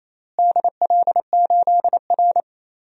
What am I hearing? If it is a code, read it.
Morse code, heard as a keyed tone, about 28 wpm, 700 Hz. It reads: DL8R
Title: DXCC-100 Call Signs - 4 Characters - Repeated 2x - 28wpm